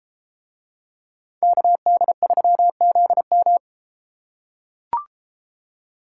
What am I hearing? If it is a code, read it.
Morse code, heard as a keyed tone, about 31 wpm, 700 Hz. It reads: KD3ZM